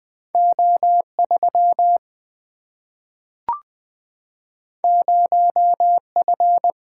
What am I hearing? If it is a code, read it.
Morse code, heard as a keyed tone, about 20 wpm, 700 Hz. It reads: O3 0F
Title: Single-Letter Single-Number 20wpm